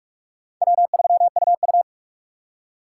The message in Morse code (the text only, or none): W3UU